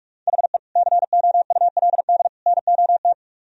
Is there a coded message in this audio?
Yes: Morse code from 0.3 to 3.1 s, heard as a keyed tone, about 44 wpm, 700 Hz. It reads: HE COULD NOT